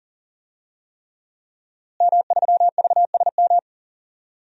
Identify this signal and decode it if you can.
Morse code, heard as a keyed tone, about 40 wpm, 700 Hz. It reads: M3VSM
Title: DXCC-100 Call Signs - 5 Characters - Repeated 3x - 40wpm